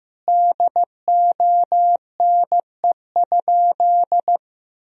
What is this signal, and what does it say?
Morse code, heard as a keyed tone, about 15 wpm, 700 Hz. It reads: DONE?
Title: Sentences from Top 1000 Words - Review Part 4 - 15wpm